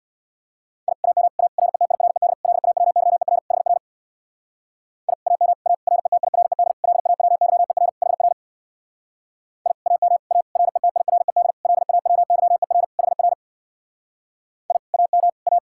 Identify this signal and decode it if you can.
Morse code, heard as a keyed tone, about 62 wpm, 700 Hz. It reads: I AM A RETIRED LAWYER HR I AM A RETIRED LAWYER HR I AM A RETIRED LAWYER HR I AM A
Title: Speed-Racing Common QSO Phrases Encore - 50wpm